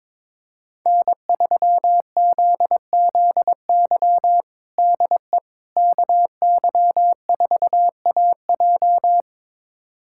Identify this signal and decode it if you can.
Morse code, heard as a keyed tone, about 22 wpm, 700 Hz. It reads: N3ZZY DE KY4AJ
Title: USA Callsign DE Callsign - 5 Characters - 22wpm